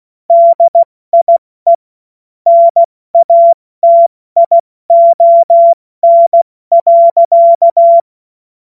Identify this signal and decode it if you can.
Morse code, heard as a keyed tone, about 16 wpm, 680 Hz. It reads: DIE NATION.
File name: Morsekurs / Witze / Witz 09. Februar 2021, 70 F80